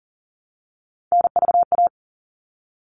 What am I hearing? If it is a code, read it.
Morse code, heard as a keyed tone, about 40 wpm, 700 Hz. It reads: NVA
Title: All Cut Numbers - 3-Digit 40wpm